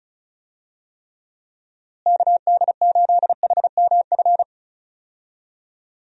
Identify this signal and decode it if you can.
Morse code, heard as a keyed tone, about 36 wpm, 700 Hz. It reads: KD8HMF
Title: USA Call Signs - 6 Characters - Repeated 2x - 35wpm